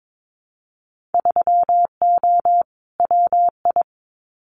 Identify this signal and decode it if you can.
Morse code, heard as a keyed tone, about 22 wpm, 700 Hz. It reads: 3O WI
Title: Winter Field Day Exchanges 22wpm